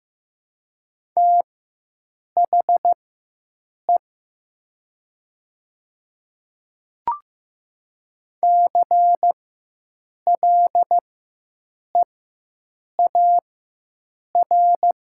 Morse code, heard as a keyed tone, about 15 wpm, spaced as for 7 wpm, 700 Hz. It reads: THE CLEAR
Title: 4x Character Spacing - Top 500 Words - 15wpm